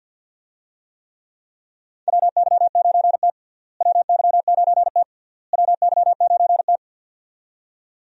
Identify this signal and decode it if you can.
Morse code, heard as a keyed tone, about 51 wpm, 700 Hz. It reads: WY9T WY9T WY9T
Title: USA Call Signs - 4 Characters - Repeated 3x - 50wpm